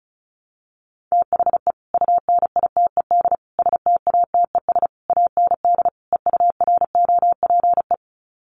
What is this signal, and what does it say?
Morse code, heard as a keyed tone, about 34 wpm, 700 Hz. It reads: THE UNITED STATES AND EUROPE
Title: Sets of 5 Words Encore 35wpm